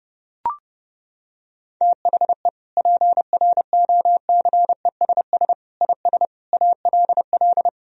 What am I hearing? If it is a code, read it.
Morse code, heard as a keyed tone, about 30 wpm, 700 Hz. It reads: THE PROCESS IS ALL
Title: Sentences from Top 1000 Words - Review Part 3 - 30wpm